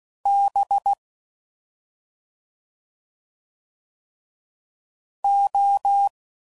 Morse code, heard as a keyed tone, about 16 wpm, 800 Hz. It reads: B O
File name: Suffixes / Special ING's / Sngl ING's